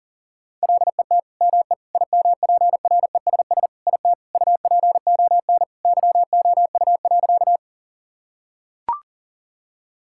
Morse code, heard as a keyed tone, about 40 wpm, 700 Hz. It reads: LET ME IMPRESS IT UPON YOU.
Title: Sentences from Top 1600 Words 40wpm